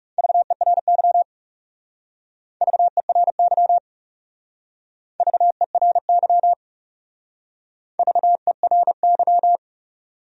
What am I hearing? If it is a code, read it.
Morse code, heard as a keyed tone, about 39 wpm, 700 Hz. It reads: VERY VERY VERY VERY